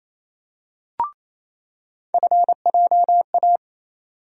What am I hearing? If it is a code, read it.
Morse code, heard as a keyed tone, about 27 wpm, 700 Hz. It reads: FJA
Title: USA Call Sign Postfixes - 3 Letters 28wpm